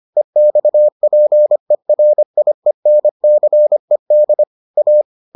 Morse code, heard as a keyed tone, about 25 wpm, 600 Hz. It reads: EXPERIENCED A